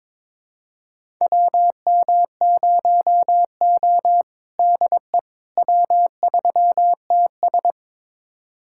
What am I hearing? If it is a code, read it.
Morse code, heard as a keyed tone, about 22 wpm, 700 Hz. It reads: WM0O DE W3TS
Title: USA Callsign DE Callsign - 4 Characters - 22wpm